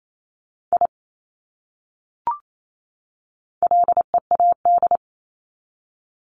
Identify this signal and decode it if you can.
Morse code, heard as a keyed tone, about 28 wpm, 700 Hz. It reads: I LEAD
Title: Top 500 Words 28wpm